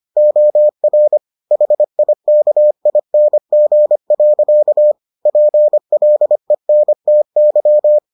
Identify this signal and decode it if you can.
Morse code, heard as a keyed tone, about 25 wpm, 600 Hz. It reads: OR HIKING. PLENTY